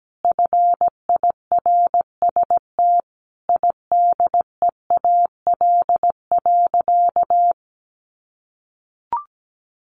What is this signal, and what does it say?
Morse code, heard as a keyed tone, about 17 wpm, 700 Hz. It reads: FIRST IDEAL.
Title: Sentences from Top 1400 Words 17wpm